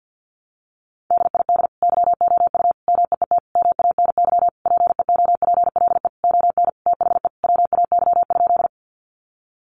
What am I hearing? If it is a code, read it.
Morse code, heard as a keyed tone, about 50 wpm, 700 Hz. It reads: DID YOU MEET MANY PEOPLE ON THE WAY?
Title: Sentences from Top 1100 Words 50wpm